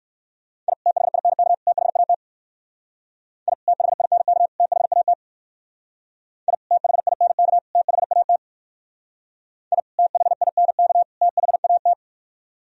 Morse code, heard as a keyed tone, about 56 wpm, 700 Hz. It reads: I THINK THAT I THINK THAT I THINK THAT I THINK THAT
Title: Speed-Racing - Sets of 3 Words 45wpm